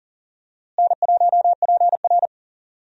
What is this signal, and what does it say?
Morse code, heard as a keyed tone, about 40 wpm, 700 Hz. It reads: N1PR
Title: USA Call Signs - 4 Characters - Repeated 2x - 40wpm